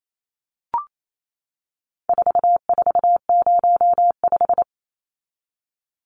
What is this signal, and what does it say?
Morse code, heard as a keyed tone, about 28 wpm, 700 Hz. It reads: 4405
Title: Numbers - 4-Digit 28wpm